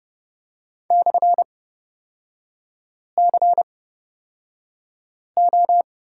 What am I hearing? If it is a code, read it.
Morse code, heard as a keyed tone, about 30 wpm, spaced as for 5 wpm, 700 Hz. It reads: /CO
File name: Lesson 38e - Letter q - Three Characters 30wpm